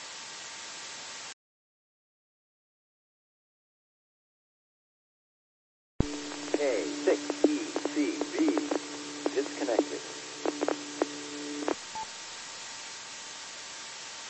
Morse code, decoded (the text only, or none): none